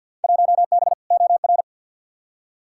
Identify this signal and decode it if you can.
Morse code, heard as a keyed tone, about 52 wpm, 700 Hz. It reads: 1B OR
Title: Field Day Exchanges Encore - 50wpm